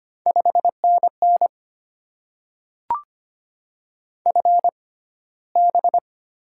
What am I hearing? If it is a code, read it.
Morse code, heard as a keyed tone, about 25 wpm, spaced as for 15 wpm, 700 Hz. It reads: 5NN FB
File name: Top 20 QSO Elements - 6x Character Spacing - 25wpm